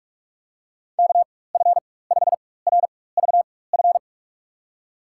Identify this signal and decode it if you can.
Morse code, heard as a keyed tone, about 41 wpm, spaced as for 20 wpm, 700 Hz. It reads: KF5RVF
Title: USA Call Signs - 6 Characters - 4x Character Spacing - 45wpm